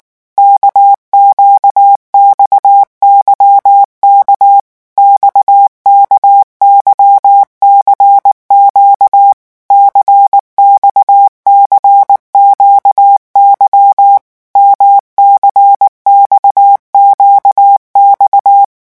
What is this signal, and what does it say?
Morse code, heard as a keyed tone, about 19 wpm, 800 Hz. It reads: KQXYK XKYCQ CXCQY MCXQX
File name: revision1-vitesse 18 mots